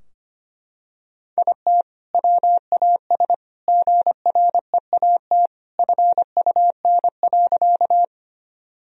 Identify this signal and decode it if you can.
Morse code, heard as a keyed tone, about 25 wpm, 700 Hz. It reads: IT WAS GREAT FUN.
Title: Sentences from Top 1000 Words - Review Part 4 - 25wpm